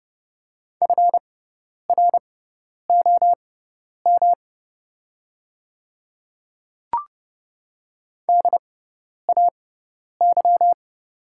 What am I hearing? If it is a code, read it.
Morse code, heard as a keyed tone, about 30 wpm, spaced as for 10 wpm, 690 Hz. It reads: FROM DAY
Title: 6x Character Spacing - Top 100 Words - 30wpm